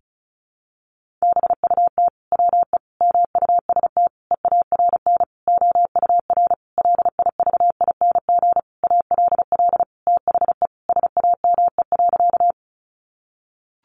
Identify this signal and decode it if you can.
Morse code, heard as a keyed tone, about 34 wpm, 700 Hz. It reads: BUT WE MUST EARN OUR LIVING ALL THE SAME.
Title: Sentences from Top 1500 Words 35wpm